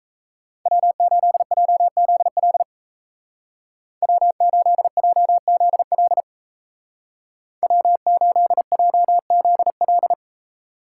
Morse code, heard as a keyed tone, about 34 wpm, 700 Hz. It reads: W8JZL W8JZL W8JZL
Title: Speed Racing - USA 1x3 Call Signs 28wpm